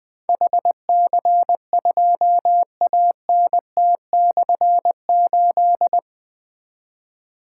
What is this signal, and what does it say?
Morse code, heard as a keyed tone, about 20 wpm, 700 Hz. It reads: HC2ANT/8
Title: Difficult Call Signs 20wpm